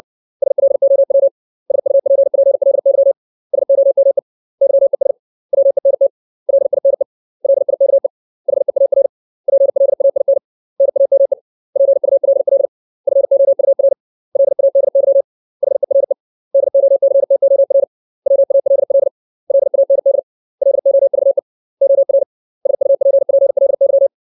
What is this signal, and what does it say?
Morse code, heard as a keyed tone, about 60 wpm, 550 Hz. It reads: SLOW SUPPLY SOME YES WENT LINE LIKE HEAR PLAIN NAME PULL FOUR LADY HERE DOCTOR WALL LAND LOVE OR SUPPLY
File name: 632 words / 632 Words 60WPM